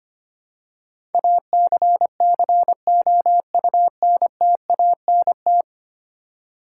Morse code, heard as a keyed tone, about 25 wpm, 700 Hz. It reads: ACCOUNTANT
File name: 10 Letter Words - First 3 Letters Given - 25wpm